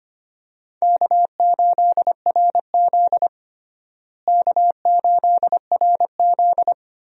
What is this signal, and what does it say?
Morse code, heard as a keyed tone, about 25 wpm, 700 Hz. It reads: K8RZ K8RZ
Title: USA Call Signs - 4 Characters - Repeated 2x - 25wpm